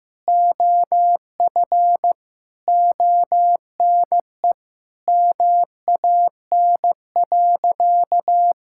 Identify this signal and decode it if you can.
Morse code, heard as a keyed tone, about 15 wpm, 700 Hz. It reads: OF ONE MAN.
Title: Sentences from Top 1000 Words - Review Part 6 - 15wpm